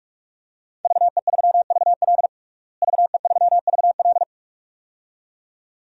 Morse code, heard as a keyed tone, about 46 wpm, 700 Hz. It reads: VE3VL VE3VL
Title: DXCC-100 Call Signs - 5 Characters - Repeated 2x - 45wpm